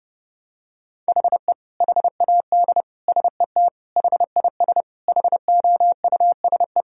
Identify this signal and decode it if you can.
Morse code, heard as a keyed tone, about 32 wpm, 700 Hz. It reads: HE HAD SET HIS HOUSE
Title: Sentences from Top 1000 Words - Review Part 2 - 30wpm